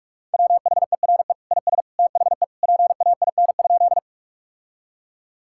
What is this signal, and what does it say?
Morse code, heard as a keyed tone, about 43 wpm, 700 Hz. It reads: WHERE IS THE PAIN?
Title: Sentences from Top 1600 Words 45wpm